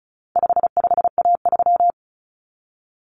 Morse code, heard as a keyed tone, about 34 wpm, 700 Hz. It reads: 55A3
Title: Common Cut Numbers - 4-Digit 35wpm